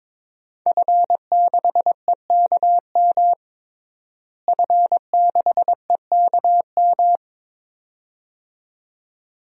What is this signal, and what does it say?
Morse code, heard as a keyed tone, about 22 wpm, 700 Hz. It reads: F6EKM F6EKM
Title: DXCC-100 Call Signs - 5 Characters - Repeated 2x - 22wpm